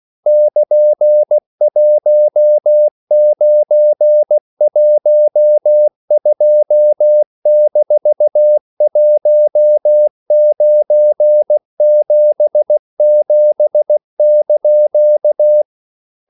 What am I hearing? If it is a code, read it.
Morse code, heard as a keyed tone, about 16 wpm, 600 Hz. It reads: <KN>1912-1977)
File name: CW / Practice / EnglishQuotations0416